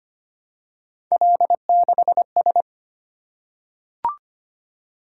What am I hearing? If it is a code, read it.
Morse code, heard as a keyed tone, about 25 wpm, 700 Hz. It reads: L6S
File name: Three Letter-Number Combinations - 25wpm